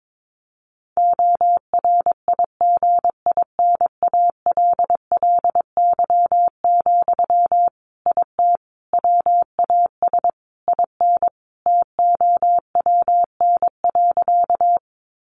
Morse code, heard as a keyed tone, about 22 wpm, 700 Hz. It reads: ORIGINALLY, IT WAS IN TOWN.
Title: Sentences from Top 1500 Words 22wpm